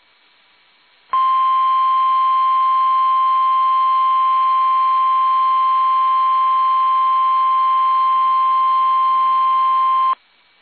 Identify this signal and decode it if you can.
Emergency Alert System (SAME) tones (the 1050 Hz attention tone). It is heard from 1.1 to 10.2 s.
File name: eas / KUSO / KUSO